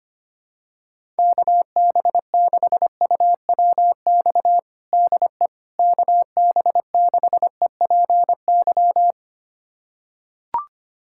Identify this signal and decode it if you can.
Morse code, heard as a keyed tone, about 25 wpm, 700 Hz. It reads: KB6UWX DE KB6EPY
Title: USA Callsign DE Callsign 25wpm